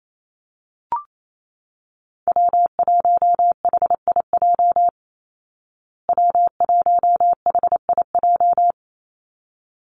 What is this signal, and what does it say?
Morse code, heard as a keyed tone, about 28 wpm, 700 Hz. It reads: W1HIJ W1HIJ
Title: Call Signs Repeated 3x - 28wpm